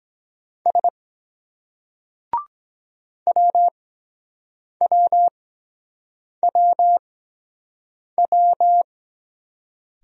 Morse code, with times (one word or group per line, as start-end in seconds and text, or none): 0.7-0.9 S
3.3-8.8 WWWW